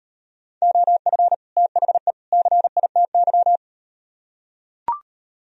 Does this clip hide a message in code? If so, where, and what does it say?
Yes: Morse code from 0.6 to 3.6 s, heard as a keyed tone, about 38 wpm, 700 Hz. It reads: OF THE CITY